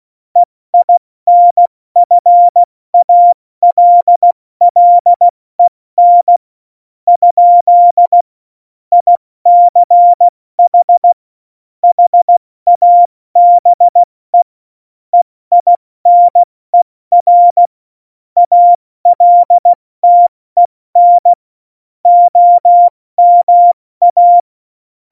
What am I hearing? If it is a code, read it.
Morse code, heard as a keyed tone, about 16 wpm, spaced as for 14 wpm, 700 Hz. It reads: EINFALLEN? ICH HABE EINER ALTEN OMA
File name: Morsekurs / Witze / Witz 30. April 2019, 70 F80